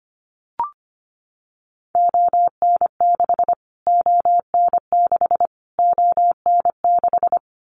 Morse code, heard as a keyed tone, about 25 wpm, 700 Hz. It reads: ON6 ON6 ON6
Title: Callsign Prefixes for Top 25 DXCC - Repeated 3x - 25wpm